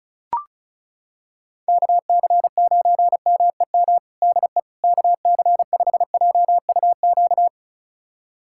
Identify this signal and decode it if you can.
Morse code, heard as a keyed tone, about 34 wpm, 700 Hz. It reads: KC9MEM DE KC5JUQ
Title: USA Callsign DE Callsign - 6 Characters - 35wpm